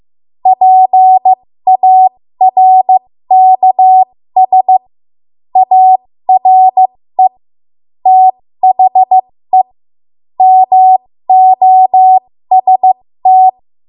Morse code, heard as a keyed tone, about 15 wpm, spaced as for 13 wpm, 750 Hz. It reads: PARKS ARE THE MOST
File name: Archive / 13 WPM / 13WPM